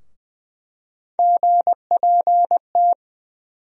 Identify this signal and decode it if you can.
Morse code, heard as a keyed tone, about 20 wpm, 700 Hz. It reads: GPT
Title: USA Call Sign Postfixes - 3 Letters 20wpm